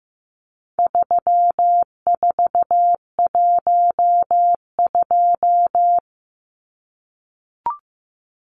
Morse code, heard as a keyed tone, about 15 wpm, 700 Hz. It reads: 3412
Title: Numbers - 4-Digit 15wpm